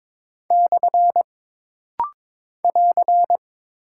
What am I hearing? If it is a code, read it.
Morse code, heard as a keyed tone, about 22 wpm, 700 Hz. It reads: / <AR>
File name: practice / 22wpm / Single Punctuation-Prosign - Rapid-Fire - 22wpm